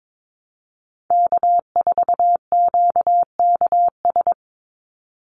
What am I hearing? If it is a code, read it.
Morse code, heard as a keyed tone, about 22 wpm, 700 Hz. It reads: K4QKS
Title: USA Call Signs 22wpm